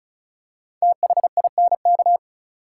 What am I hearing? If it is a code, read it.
Morse code, heard as a keyed tone, about 36 wpm, 700 Hz. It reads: THINK
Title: Five Letter Words - 35wpm